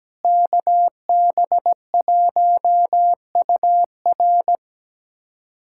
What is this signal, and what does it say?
Morse code, heard as a keyed tone, about 17 wpm, 700 Hz. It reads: KB1UR
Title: USA Callsign DE Callsign - 5 Characters - 17wpm